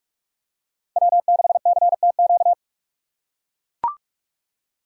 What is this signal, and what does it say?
Morse code, heard as a keyed tone, about 41 wpm, 700 Hz. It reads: W6CTQ
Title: USA Call Signs - 5 Characters - 6x Character Spacing - 45wpm